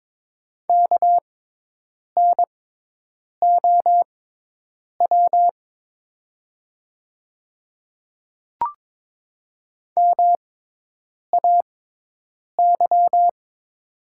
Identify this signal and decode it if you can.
Morse code, heard as a keyed tone, about 22 wpm, spaced as for 8 wpm, 700 Hz. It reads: KNOW MAY
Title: 6x Character Spacing - Top 100 Words - 22wpm